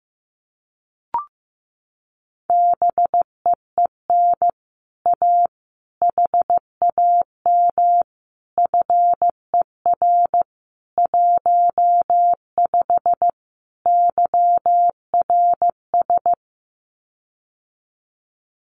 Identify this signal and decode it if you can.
Morse code, heard as a keyed tone, about 15 wpm, 700 Hz. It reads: BEEN A HAM FER 15 YRS
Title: Common QSO Phrases Encore 15wpm